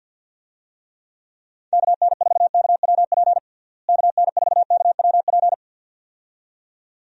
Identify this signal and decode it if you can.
Morse code, heard as a keyed tone, about 49 wpm, 700 Hz. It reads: KN4KWP KN4KWP
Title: USA Call Signs - 6 Characters - Repeated 2x - 50wpm